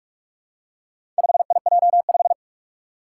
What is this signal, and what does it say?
Morse code, heard as a keyed tone, about 44 wpm, 700 Hz. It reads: 5IJ5
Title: Four Challenging Letter-Number Combinations 45wpm